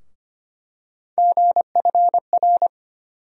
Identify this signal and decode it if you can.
Morse code, heard as a keyed tone, about 25 wpm, 700 Hz. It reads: GFR
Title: USA Call Sign Postfixes - 3 Letters 25wpm